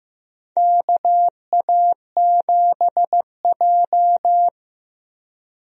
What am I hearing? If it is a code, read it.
Morse code, heard as a keyed tone, about 15 wpm, 700 Hz. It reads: KA7J
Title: USA Call Signs - 4 Characters - Repeated 2x - 15wpm